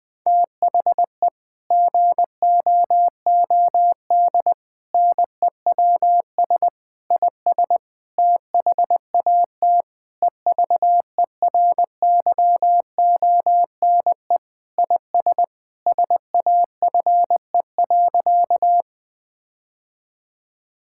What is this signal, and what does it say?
Morse code, heard as a keyed tone, about 21 wpm, 700 Hz. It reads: THE GOOD NEWS IS THAT EVERYONE IS SAFE.
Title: Sentences with 5-Word Phrases - 20wpm